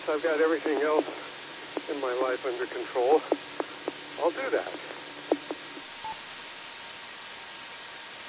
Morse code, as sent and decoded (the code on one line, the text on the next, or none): none